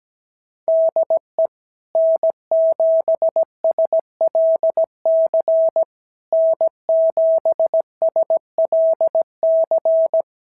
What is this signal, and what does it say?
Morse code, heard as a keyed tone, about 17 wpm, 650 Hz. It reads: DE N7SLC N7SLC
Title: SOTA QSO Practice 17wpm